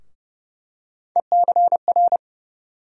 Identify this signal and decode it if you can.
Morse code, heard as a keyed tone, about 30 wpm, 700 Hz. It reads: ECR
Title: USA Call Sign Postfixes - 3 Letters 30wpm